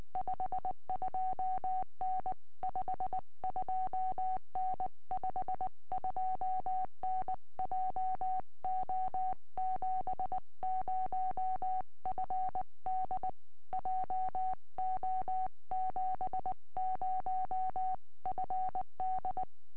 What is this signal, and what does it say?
Morse code, heard as a keyed tone, about 20 wpm, 750 Hz. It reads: 52N 52N 52N JO70FD JO70FD
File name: tropo / f6ape 2m